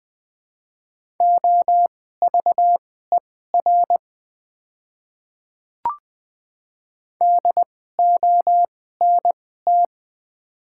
Morse code, heard as a keyed tone, about 20 wpm, spaced as for 14 wpm, 700 Hz. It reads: OVER DONT
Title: 2x Character Spacing - Top 300 Words - 20wpm